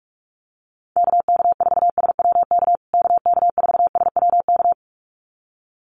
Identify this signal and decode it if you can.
Morse code, heard as a keyed tone, about 44 wpm, 700 Hz. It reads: KK4SWK KK4SWK
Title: USA Call Signs Repeated 2x 45wpm